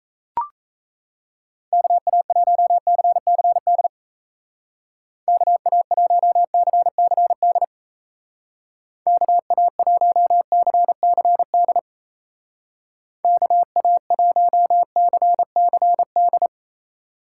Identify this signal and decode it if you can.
Morse code, heard as a keyed tone, about 34 wpm, 700 Hz. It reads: KA1CCD KA1CCD KA1CCD KA1CCD
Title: Speed Racing - USA 2x3 Call Signs 28wpm